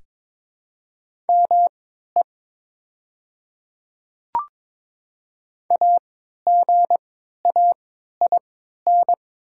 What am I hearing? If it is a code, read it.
Morse code, heard as a keyed tone, about 22 wpm, spaced as for 13 wpm, 700 Hz. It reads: ME AGAIN
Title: 3x Character Spacing - Top 200 Words - 22wpm